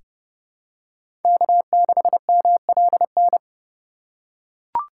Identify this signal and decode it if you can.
Morse code, heard as a keyed tone, about 30 wpm, 700 Hz. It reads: K6MLN
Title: USA Call Signs - 4x Character Spacing - 30wpm